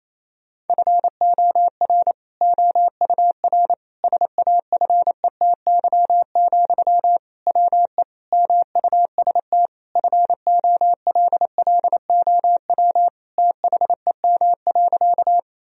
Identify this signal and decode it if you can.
Morse code, heard as a keyed tone, about 28 wpm, 700 Hz. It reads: FOR OUR SAFETY, WE MUST FOLLOW THEM.
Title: Sentences from Top 1000 Words - Review Part 4 - 28wpm